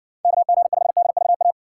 Morse code, heard as a keyed tone, about 59 wpm, 700 Hz. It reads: KG5DVA